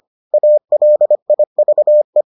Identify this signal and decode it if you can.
Morse code, heard as a keyed tone, about 25 wpm, 600 Hz. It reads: ALIVE